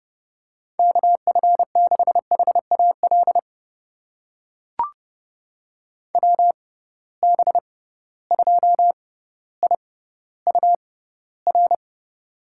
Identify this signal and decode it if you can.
Morse code, heard as a keyed tone, about 30 wpm, spaced as for 15 wpm, 700 Hz. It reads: KF6HAL WB2IUR
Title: USA Call Signs - 6 Characters - 6x Character Spacing - 30wpm